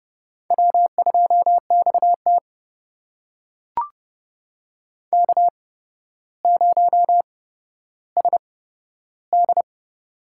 Morse code, heard as a keyed tone, about 30 wpm, spaced as for 13 wpm, 700 Hz. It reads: W2XT K0SD
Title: USA Call Signs - 4 Characters - 8x Character Spacing - 30wpm